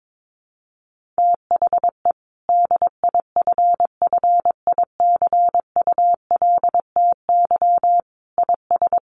Morse code, heard as a keyed tone, about 22 wpm, 700 Hz. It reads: THE DIFFICULTY IS
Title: Sentences from Top 1500 Words 22wpm